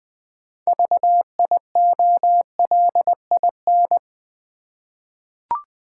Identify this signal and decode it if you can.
Morse code, heard as a keyed tone, about 20 wpm, 700 Hz. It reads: VIOLIN
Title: Easy Words with Hints - 20wpm